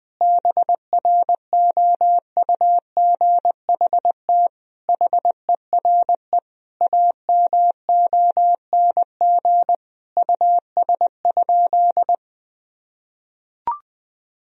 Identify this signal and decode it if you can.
Morse code, heard as a keyed tone, about 19 wpm, 700 Hz. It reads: BROUGHT HERE AMONG US?
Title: Sentences from Top 1000 Words - Review Part 3 - 20wpm